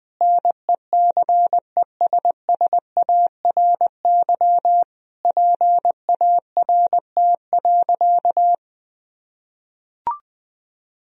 Morse code, heard as a keyed tone, about 20 wpm, 700 Hz. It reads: NECESSARY PART.
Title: Sentences from Top 1000 Words - Review Part 1 - 20wpm